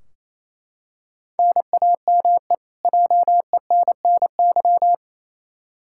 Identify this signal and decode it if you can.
Morse code, heard as a keyed tone, about 28 wpm, 700 Hz. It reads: NAME JENNY
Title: K1USN SST QSO Practice 28wpm